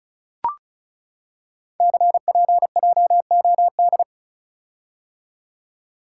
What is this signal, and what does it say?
Morse code, heard as a keyed tone, about 36 wpm, 700 Hz. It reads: CPJOD
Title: Code Groups - 5 Letters - 35wpm